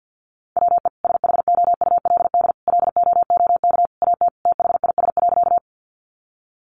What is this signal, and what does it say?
Morse code, heard as a keyed tone, about 50 wpm, 700 Hz. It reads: WE SHOULD LOOK AT THIS.
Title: Sentences from Top 300 Words 50wpm